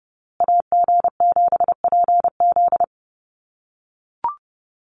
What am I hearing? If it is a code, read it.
Morse code, heard as a keyed tone, about 30 wpm, 700 Hz. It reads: AG7PZ
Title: USA Call Signs Repeated 2x 30wpm